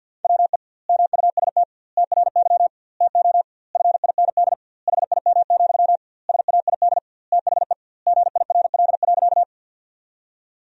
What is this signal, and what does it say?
Morse code, heard as a keyed tone, about 50 wpm, 700 Hz. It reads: WE MUST TRY TO FIND HIM, SAID THE GIRL.
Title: 2x Word-Spacing - Sentences from Top 400 Words - 50wpm